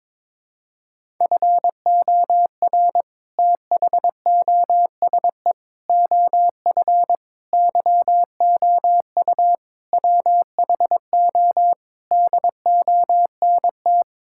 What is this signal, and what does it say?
Morse code, heard as a keyed tone, about 21 wpm, 700 Hz. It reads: FOR THOSE OF YOU WHO DONT
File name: practice / 22wpm / Sets of 7 Words - 22wpm